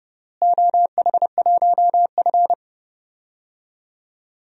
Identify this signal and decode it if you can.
Morse code, heard as a keyed tone, about 30 wpm, 700 Hz. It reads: OH1F
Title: DXCC-100 Call Signs - 4 Characters - Repeated 3x - 30wpm